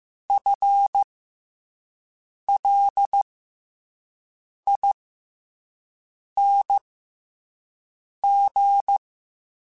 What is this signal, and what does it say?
Morse code, heard as a keyed tone, about 15 wpm, spaced as for 5 wpm, 800 Hz. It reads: FLING
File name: FL / FL mp3s / FL Prefixes 3x2 @5